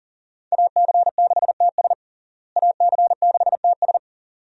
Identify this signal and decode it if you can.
Morse code, heard as a keyed tone, about 40 wpm, 700 Hz. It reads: AC6TS AC6TS
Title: USA Call Signs - 5 Characters - Repeated 2x - 40wpm